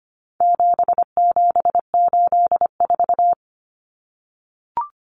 Morse code, heard as a keyed tone, about 25 wpm, 700 Hz. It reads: 7784
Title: Common Cut Numbers - 4-Digit 25wpm